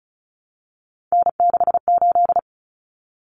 Morse code, heard as a keyed tone, about 34 wpm, 700 Hz. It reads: N68
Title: Common Cut Numbers - 3-Digit 35wpm